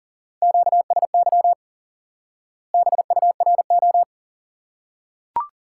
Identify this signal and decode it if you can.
Morse code, heard as a keyed tone, about 40 wpm, 700 Hz. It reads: QSY BURO
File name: IQE Course 3 - Lesson 89 - YAGI - Two Elements - 40wpm